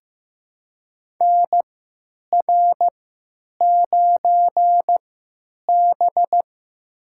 Morse code, heard as a keyed tone, about 15 wpm, spaced as for 9 wpm, 700 Hz. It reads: NR9B
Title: USA Call Signs - 3x Character Spacing - 15wpm